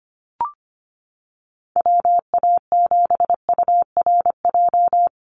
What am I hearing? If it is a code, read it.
Morse code, heard as a keyed tone, about 25 wpm, 700 Hz. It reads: WA7URJ
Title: USA Call Signs 25wpm